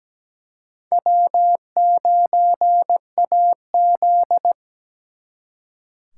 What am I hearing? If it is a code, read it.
Morse code, heard as a keyed tone, about 20 wpm, 700 Hz. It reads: W9AZ
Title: Speed Racing - USA 1x2 Call Signs 17wpm